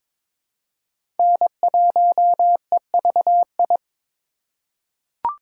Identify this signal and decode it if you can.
Morse code, heard as a keyed tone, about 23 wpm, 700 Hz. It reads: N1EVI
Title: USA Call Signs - 5 Characters - 3x Character Spacing - 22wpm